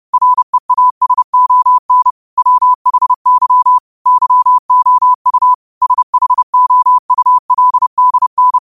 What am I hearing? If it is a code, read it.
Morse code, heard as a keyed tone, about 30 wpm, 1000 Hz. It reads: REASON WHY YOU SHOULDN
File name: qotdfiles / 30wpm